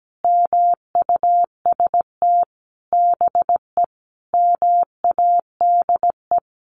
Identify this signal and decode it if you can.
Morse code, heard as a keyed tone, about 17 wpm, 700 Hz. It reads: MUST BE MADE
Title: Sentences from Top 1400 Words 17wpm